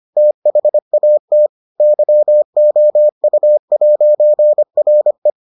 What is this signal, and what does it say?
Morse code, heard as a keyed tone, about 25 wpm, 600 Hz. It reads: THAT YOU'RE